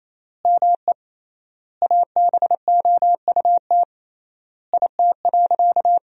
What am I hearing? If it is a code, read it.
Morse code, heard as a keyed tone, about 28 wpm, 700 Hz. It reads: ME ABOUT IT.
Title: 3x Word-Spacing - Sentences from Top 200 Words - 28wpm